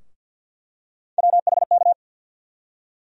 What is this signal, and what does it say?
Morse code, heard as a keyed tone, about 48 wpm, 700 Hz. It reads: WHK
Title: USA Call Sign Postfixes - 3 Letters 50wpm